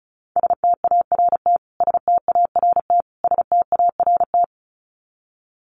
Morse code, heard as a keyed tone, about 34 wpm, 700 Hz. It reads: START START START
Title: Top 300 Words Repeated 35wpm